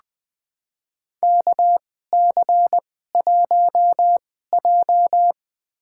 Morse code, heard as a keyed tone, about 21 wpm, spaced as for 15 wpm, 700 Hz. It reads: KC1J
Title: USA Call Signs - 4 Characters - 2x Character Spacing - 20wpm